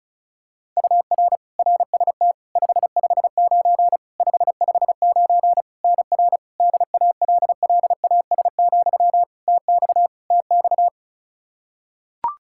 Morse code, heard as a keyed tone, about 34 wpm, 700 Hz. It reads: UR RST 559 559 NR DALLAS, TX TX
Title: Practice QSO - Exchange 1 - Beginning - 35wpm